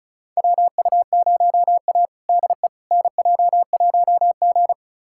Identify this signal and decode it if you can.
Morse code, heard as a keyed tone, about 36 wpm, 700 Hz. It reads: WU0A DE NJ1G
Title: USA Callsign DE Callsign - 4 Characters - 35wpm